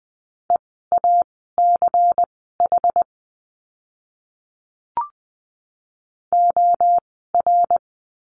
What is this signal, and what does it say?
Morse code, heard as a keyed tone, about 20 wpm, spaced as for 15 wpm, 700 Hz. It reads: EACH OR
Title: 146 Lesson 24f - Letter y - Words 20wpm